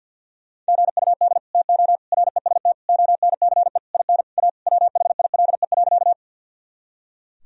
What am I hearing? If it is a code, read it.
Morse code, heard as a keyed tone, about 50 wpm, 700 Hz. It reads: GUD TO REST ONCE IN A WHILE.
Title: Sentences from Top 500 Words and Abbreviations 50wpm